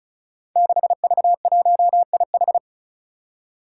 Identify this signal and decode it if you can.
Morse code, heard as a keyed tone, about 36 wpm, 700 Hz. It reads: 6V1IH
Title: Five Challenging Letter-Number Combinations 35wpm